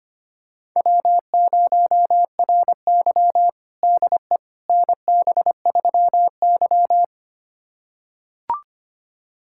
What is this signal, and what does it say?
Morse code, heard as a keyed tone, about 25 wpm, 700 Hz. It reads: W0RY DE NB3Y
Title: USA Callsign DE Callsign - 4 Characters - 25wpm